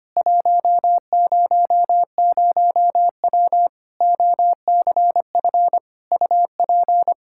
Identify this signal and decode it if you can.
Morse code, heard as a keyed tone, about 25 wpm, 700 Hz. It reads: 100W OCF UP
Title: K1USN SST QSO Practice 25wpm